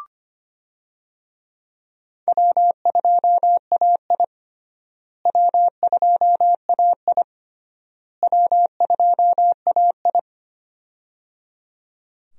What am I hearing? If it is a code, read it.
Morse code, heard as a keyed tone, about 25 wpm, 700 Hz. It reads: W2AI W2AI W2AI
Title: USA Call Signs - 4 Characters - Repeated 3x - 25wpm